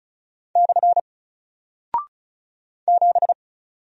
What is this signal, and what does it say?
Morse code, heard as a keyed tone, about 36 wpm, 700 Hz. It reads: / 7
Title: Single Character (Letter:Number:Punctuation:Prosign) Alt 35wpm